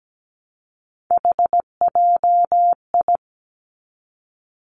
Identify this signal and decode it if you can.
Morse code, heard as a keyed tone, about 17 wpm, 700 Hz. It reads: HJI
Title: Three Challenging Letter-Number Combinations 17wpm